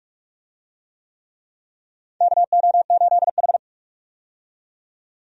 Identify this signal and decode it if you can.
Morse code, heard as a keyed tone, about 46 wpm, 700 Hz. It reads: KO8H
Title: USA Call Signs - 4 Characters - 45wpm